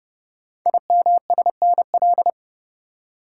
Morse code, heard as a keyed tone, about 28 wpm, 700 Hz. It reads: IMSNL
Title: Code Groups - 5 Letters - 30wpm